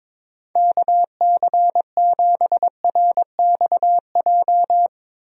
Morse code, heard as a keyed tone, about 22 wpm, 700 Hz. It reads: KC7RXJ
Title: USA Call Signs - 6 Characters - Repeated 3x - 22wpm